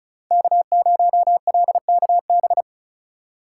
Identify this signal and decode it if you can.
Morse code, heard as a keyed tone, about 36 wpm, 700 Hz. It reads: K0LKB
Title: USA Call Signs - 4x Character Spacing - 35wpm